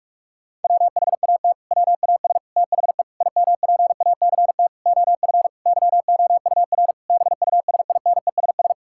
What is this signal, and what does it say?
Morse code, heard as a keyed tone, about 46 wpm, 700 Hz. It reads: WHAT WAS THE IMPACT OF YOUR BUSINESS
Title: Sentences from Top 1000 Words - Review Part 6 - 45wpm